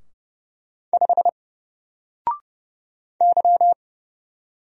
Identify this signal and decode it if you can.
Morse code, heard as a keyed tone, about 30 wpm, 700 Hz. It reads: 5 Y
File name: Single-Letter-Number-30wpm